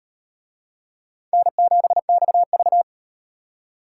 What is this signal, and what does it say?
Morse code, heard as a keyed tone, about 33 wpm, 700 Hz. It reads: N7XV